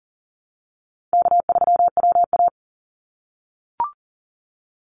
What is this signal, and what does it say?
Morse code, heard as a keyed tone, about 40 wpm, 700 Hz. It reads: K3WA
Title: Easy Call Signs - 40wpm